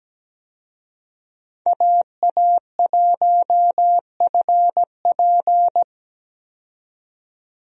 Morse code, heard as a keyed tone, about 17 wpm, 700 Hz. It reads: AA1FP
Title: USA Call Signs - 5 Characters - 17wpm